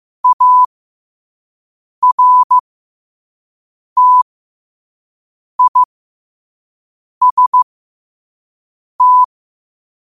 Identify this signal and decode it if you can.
Morse code, heard as a keyed tone, about 15 wpm, spaced as for 5 wpm, 1000 Hz. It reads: ARTIST
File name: qotdfiles / 5wpm